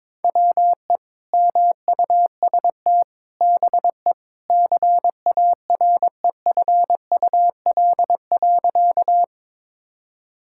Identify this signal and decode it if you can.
Morse code, heard as a keyed tone, about 22 wpm, 700 Hz. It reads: WE MUST BE CAREFUL.
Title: Sentences from Top 1600 Words 22wpm